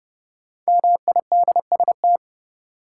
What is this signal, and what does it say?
Morse code, heard as a keyed tone, about 30 wpm, 700 Hz. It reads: MIDST
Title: Sets of 4 Words - New Word Drill - 30wpm